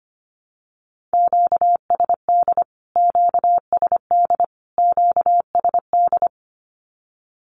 Morse code, heard as a keyed tone, about 25 wpm, 700 Hz. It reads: QSD QSD QSD
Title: Q-Codes Repeated 3x 25wpm